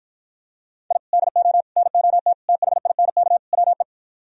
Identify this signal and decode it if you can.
Morse code, heard as a keyed tone, about 53 wpm, 700 Hz. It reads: I DO NOT THINK WE